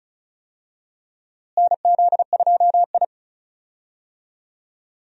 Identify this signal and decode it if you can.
Morse code, heard as a keyed tone, about 36 wpm, 700 Hz. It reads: NZ2I
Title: USA Call Signs - 4 Characters - Repeated 2x - 35wpm